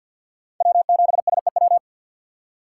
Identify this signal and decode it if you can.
Morse code, heard as a keyed tone, about 53 wpm, 700 Hz. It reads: W7SEW
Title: USA Call Signs - 5 Characters - 4x Character Spacing - 50wpm